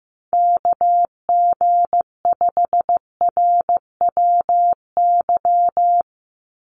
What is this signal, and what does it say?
Morse code, heard as a keyed tone, about 15 wpm, 700 Hz. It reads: KG5RWY
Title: USA Call Signs 15wpm